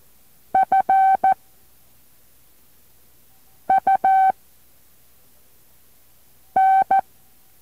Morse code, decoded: F U N